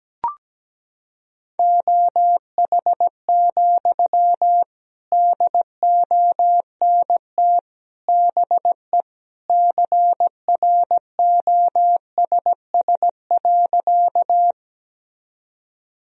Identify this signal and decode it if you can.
Morse code, heard as a keyed tone, about 17 wpm, 700 Hz. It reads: OH, DONT BE CROSS.
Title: Sentences from Top 1000 Words - Review Part 1 - 17wpm